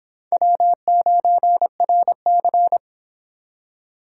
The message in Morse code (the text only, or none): W9RC